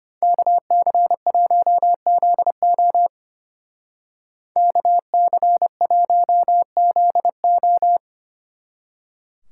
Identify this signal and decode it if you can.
Morse code, heard as a keyed tone, about 31 wpm, 700 Hz. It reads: KC1ZO KC1ZO
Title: Speed Racing - USA 2x2 Call Signs 25wpm